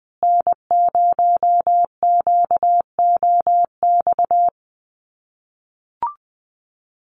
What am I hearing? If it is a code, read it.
Morse code, heard as a keyed tone, about 20 wpm, 700 Hz. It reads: N0QOX
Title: USA Call Signs Repeated 3x 20wpm